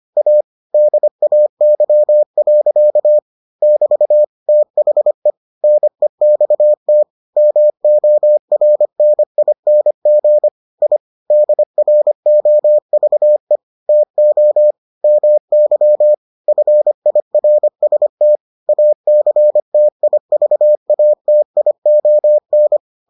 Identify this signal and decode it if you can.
Morse code, heard as a keyed tone, about 25 wpm, 600 Hz. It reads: A DAY. <BT> THE NEXT MORNING I DROVE TO MY FIRST ACTIVATION